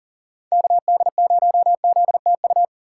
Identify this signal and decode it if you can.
Morse code, heard as a keyed tone, about 40 wpm, 700 Hz. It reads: KD0ZTU
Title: USA Call Signs - 3x Character Spacing - 40wpm